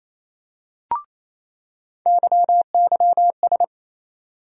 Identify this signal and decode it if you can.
Morse code, heard as a keyed tone, about 28 wpm, 700 Hz. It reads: YYS
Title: USA Call Sign Postfixes - 3 Letters 28wpm